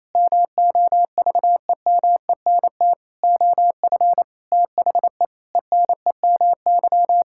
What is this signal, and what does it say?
Morse code, heard as a keyed tone, about 28 wpm, 700 Hz. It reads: MOVEMENT OF THE ENEMY
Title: Pangram Challenge v2 - 28wpm